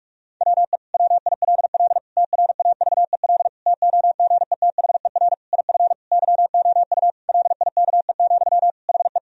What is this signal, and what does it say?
Morse code, heard as a keyed tone, about 48 wpm, 700 Hz. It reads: WE WILL TRAVEL TOGETHER IF YOU LIKE, HE
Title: Sentences from Top 1000 Words - Review Part 2 - 45wpm